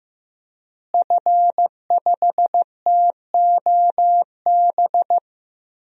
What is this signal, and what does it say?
Morse code, heard as a keyed tone, about 15 wpm, 700 Hz. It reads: F5TOB
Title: DXCC-100 Call Signs - 5 Characters - Repeated 3x - 15wpm